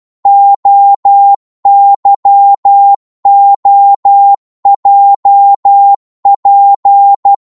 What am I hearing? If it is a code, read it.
Morse code, heard as a keyed tone, about 12 wpm, 800 Hz. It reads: OYOJP
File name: Morse / ltr-12wpm